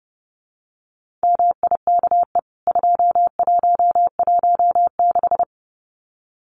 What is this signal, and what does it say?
Morse code, heard as a keyed tone, about 30 wpm, 700 Hz. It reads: MIKE 2116
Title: CWT Exchange 30wpm